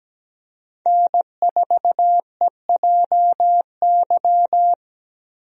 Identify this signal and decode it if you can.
Morse code, heard as a keyed tone, about 17 wpm, 700 Hz. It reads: N4EJY
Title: USA Call Signs - 5 Characters - Repeated 3x - 17wpm